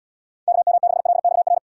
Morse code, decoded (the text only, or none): KM6LXR